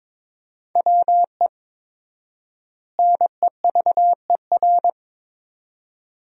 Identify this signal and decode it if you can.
Morse code, heard as a keyed tone, about 22 wpm, 700 Hz. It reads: WE NEVER
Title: 4x Word-Spacing - Sentences from Top 300 Words - 22wpm